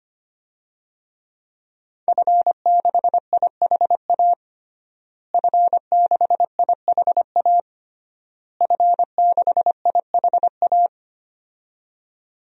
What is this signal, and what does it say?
Morse code, heard as a keyed tone, about 25 wpm, 700 Hz. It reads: F6IHA F6IHA F6IHA
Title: DXCC-100 Call Signs - 5 Characters - Repeated 3x - 25wpm